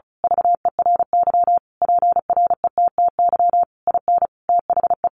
Morse code, heard as a keyed tone, about 34 wpm, 700 Hz. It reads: VERY PRETTY IN THE
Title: Sentences from Top 1300 Words 35wpm